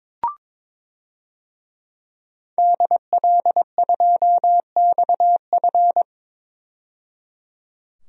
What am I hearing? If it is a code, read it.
Morse code, heard as a keyed tone, about 22 wpm, 700 Hz. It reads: DL2XF
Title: DXCC-100 Call Signs - 5 Characters - 22wpm